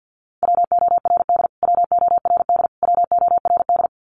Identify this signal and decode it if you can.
Morse code, heard as a keyed tone, about 48 wpm, 700 Hz. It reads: WORD WORD WORD
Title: Top 300 Words Repeated 50wpm